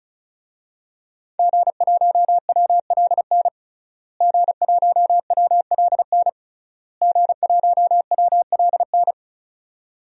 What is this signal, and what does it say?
Morse code, heard as a keyed tone, about 34 wpm, 700 Hz. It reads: G1WLN G1WLN G1WLN
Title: DXCC-100 Call Signs - 5 Characters - Repeated 3x - 35wpm